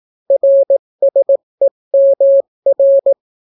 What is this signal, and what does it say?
Morse code, heard as a keyed tone, about 18 wpm, 550 Hz. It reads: RSEMR